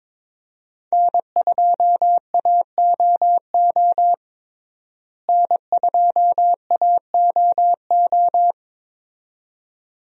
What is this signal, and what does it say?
Morse code, heard as a keyed tone, about 22 wpm, 700 Hz. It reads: N2AOO N2AOO
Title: USA Call Signs - 5 Characters - Repeated 2x - 22wpm